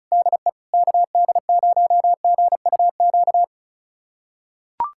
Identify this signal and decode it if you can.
Morse code, heard as a keyed tone, about 34 wpm, 700 Hz. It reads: DE KD0GUQ
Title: USA Callsign DE Callsign 35wpm